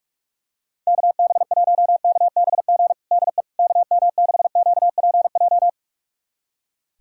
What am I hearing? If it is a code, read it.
Morse code, heard as a keyed tone, about 46 wpm, 700 Hz. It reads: KB1KBG DE KM6QPJ
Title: USA Callsign DE Callsign 45wpm